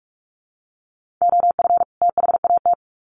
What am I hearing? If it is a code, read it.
Morse code, heard as a keyed tone, about 44 wpm, 700 Hz. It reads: OF THAT
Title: Sets of 2 Words 45wpm